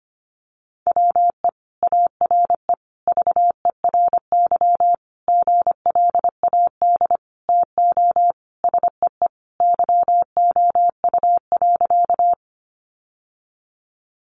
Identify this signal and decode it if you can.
Morse code, heard as a keyed tone, about 25 wpm, 700 Hz. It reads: WE ARE VERY GLAD TO SEE YOU.
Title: Sentences from Top 800 Words 25wpm